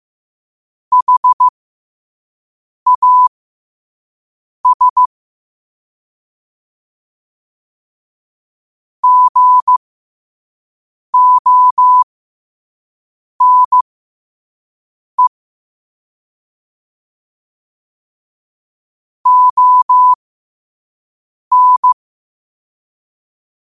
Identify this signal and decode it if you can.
Morse code, heard as a keyed tone, about 15 wpm, spaced as for 5 wpm, 1000 Hz. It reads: HAS GONE ON
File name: qotdfiles / 5wpm